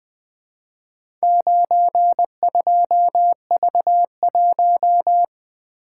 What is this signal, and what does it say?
Morse code, heard as a keyed tone, about 20 wpm, 700 Hz. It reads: 92V1
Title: Four Letter-Number Combinations - 20wpm